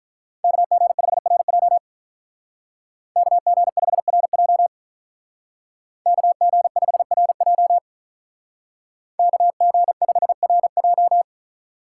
Morse code, heard as a keyed tone, about 44 wpm, 700 Hz. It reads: KG5RJ KG5RJ KG5RJ KG5RJ